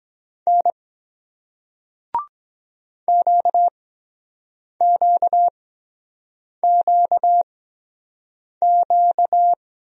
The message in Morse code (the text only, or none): N QQQQ